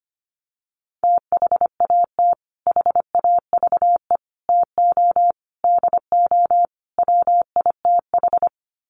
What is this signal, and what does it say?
Morse code, heard as a keyed tone, about 25 wpm, 700 Hz. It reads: THAT HAVE TO DO WITH
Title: Sets of 5 Words Encore2 25wpm